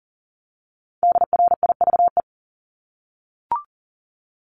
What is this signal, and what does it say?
Morse code, heard as a keyed tone, about 40 wpm, 700 Hz. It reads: DRIVE
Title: Top 500 Words Repeated 40wpm